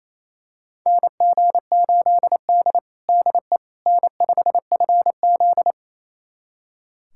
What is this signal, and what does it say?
Morse code, heard as a keyed tone, about 27 wpm, 700 Hz. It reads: NG8D DE N5FZ
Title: USA Callsign DE Callsign - 4 Characters - 28wpm